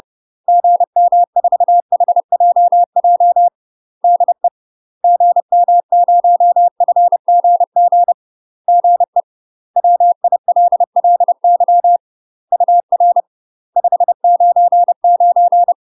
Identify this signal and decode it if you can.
Morse code, heard as a keyed tone, about 30 wpm, 700 Hz. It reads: GM4HJJ DE GM0FGG GE WILLY UR 599
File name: UK MASTER QSOs TRAINING / 30WPM / UK MASTER QSOs 0050